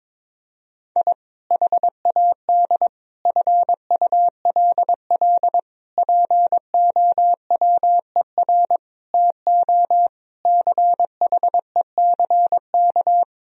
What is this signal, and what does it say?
Morse code, heard as a keyed tone, about 21 wpm, 700 Hz. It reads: I HAD FULL POWER TO CHECK
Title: Sentences from Top 1000 Words - Review Part 3 - 22wpm